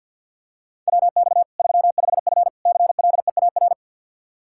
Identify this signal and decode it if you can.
Morse code, heard as a keyed tone, about 52 wpm, 700 Hz. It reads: WX 35F CLEAR
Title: K1USN SST QSO Practice 50wpm